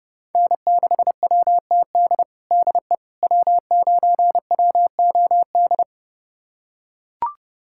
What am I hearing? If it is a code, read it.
Morse code, heard as a keyed tone, about 30 wpm, 700 Hz. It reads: N6WTD DE W9WOD
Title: USA Callsign DE Callsign - 5 Characters - 30wpm